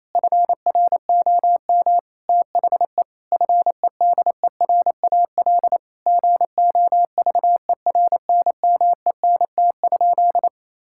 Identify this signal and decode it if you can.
Morse code, heard as a keyed tone, about 28 wpm, 700 Hz. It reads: FROM THE FEDERAL GOVERNMENT?
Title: Sentences from Top 1000 Words - Review Part 5 - 28wpm